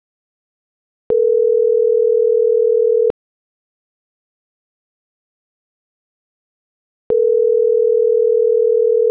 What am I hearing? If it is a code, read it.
Ringback tone (North America), 440 + 480 Hz.